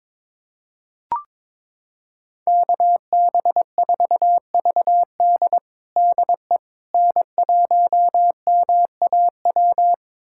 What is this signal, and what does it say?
Morse code, heard as a keyed tone, about 22 wpm, 700 Hz. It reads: KB4VD DE N1MAW
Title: USA Callsign DE Callsign - 5 Characters - 22wpm